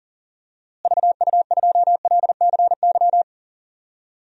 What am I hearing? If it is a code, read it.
Morse code, heard as a keyed tone, about 40 wpm, 700 Hz. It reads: VU2LCY
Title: DXCC-100 Call Signs - 6 Characters - Repeated 2x - 40wpm